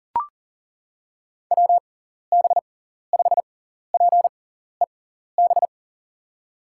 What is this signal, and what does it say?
Morse code, heard as a keyed tone, about 40 wpm, spaced as for 14 wpm, 700 Hz. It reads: WB5PEB
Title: USA Call Signs - 6 Characters - 6x Character Spacing - 40wpm